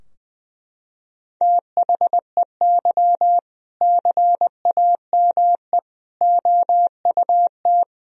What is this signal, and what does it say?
Morse code, heard as a keyed tone, about 20 wpm, 700 Hz. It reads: THEY CAME OUT
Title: Sentences from Top 1000 Words - Review Part 3 - 20wpm